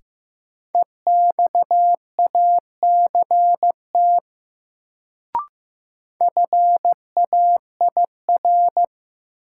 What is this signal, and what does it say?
Morse code, heard as a keyed tone, about 15 wpm, 700 Hz. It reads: EXACT FAIR
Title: Sentences from Top 1300 Words - New Word Drill Alt - 15wpm